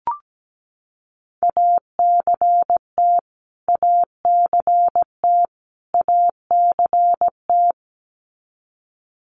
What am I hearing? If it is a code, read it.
Morse code, heard as a keyed tone, about 17 wpm, 700 Hz. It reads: ACT ACT ACT
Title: Top 300 Words Repeated 17wpm